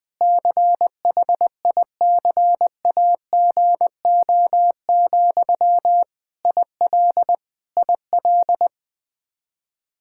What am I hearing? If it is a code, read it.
Morse code, heard as a keyed tone, about 20 wpm, 700 Hz. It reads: CHICAGO, IL IL
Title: Practice QSO - Exchange 1 - Beginning - 20wpm